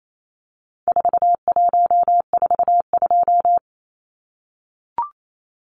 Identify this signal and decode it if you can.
Morse code, heard as a keyed tone, about 28 wpm, 700 Hz. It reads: 4142
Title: Numbers - 4-Digit 28wpm